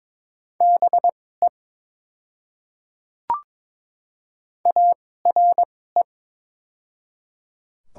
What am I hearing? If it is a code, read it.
Morse code, heard as a keyed tone, about 22 wpm, spaced as for 16 wpm, 690 Hz. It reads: BE ARE
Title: 2x Character Spacing - Top 100 Words - 22wpm